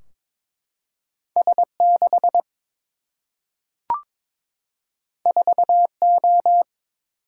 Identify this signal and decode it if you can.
Morse code, heard as a keyed tone, about 22 wpm, 700 Hz. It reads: S6 4O
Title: Single-Letter Single-Number 22wpm